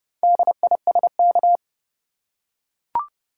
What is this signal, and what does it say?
Morse code, heard as a keyed tone, about 31 wpm, 700 Hz. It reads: DISK
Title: Four Letter Words - 30wpm